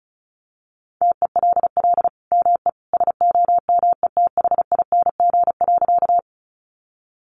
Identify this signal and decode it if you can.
Morse code, heard as a keyed tone, about 34 wpm, 700 Hz. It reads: TELL ME SOMETHING.
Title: Sentences from Top 600 Words 35wpm